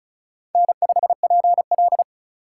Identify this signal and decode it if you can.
Morse code, heard as a keyed tone, about 36 wpm, 700 Hz. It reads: N5PL
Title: USA Call Signs - 4 Characters - 2x Character Spacing - 35wpm